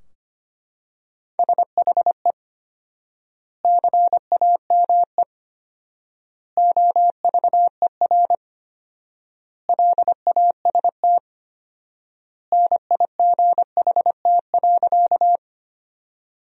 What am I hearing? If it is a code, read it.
Morse code, heard as a keyed tone, about 25 wpm, 700 Hz. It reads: SHE CAME OVER LAST NIGHT.
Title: 4x Word-Spacing - Sentences from Top 300 Words - 25wpm